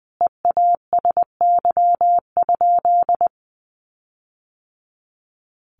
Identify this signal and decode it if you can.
Morse code, heard as a keyed tone, about 20 wpm, 700 Hz. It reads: EASY?
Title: Sentences from Top 1500 Words 20wpm